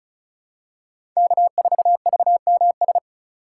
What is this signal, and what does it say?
Morse code, heard as a keyed tone, about 36 wpm, 700 Hz. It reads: K4VMS
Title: USA Call Signs - 3x Character Spacing - 35wpm